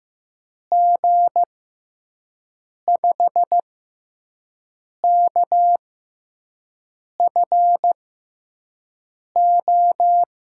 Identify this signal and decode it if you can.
Morse code, heard as a keyed tone, about 15 wpm, spaced as for 5 wpm, 700 Hz. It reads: G5KFO
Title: USA Call Signs - 6 Characters - 6x Character Spacing - 15wpm